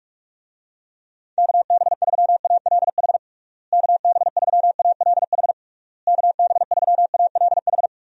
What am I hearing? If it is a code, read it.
Morse code, heard as a keyed tone, about 48 wpm, 700 Hz. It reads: KB3ALH KB3ALH KB3ALH
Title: USA Call Signs - 6 Characters - Repeated 3x - 45wpm